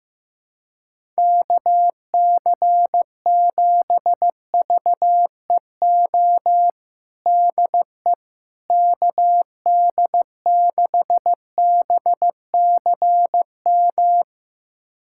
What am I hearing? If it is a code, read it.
Morse code, heard as a keyed tone, about 15 wpm, 700 Hz. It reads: KC7VEO DE KD6BCM
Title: USA Callsign DE Callsign - 6 Characters - 15wpm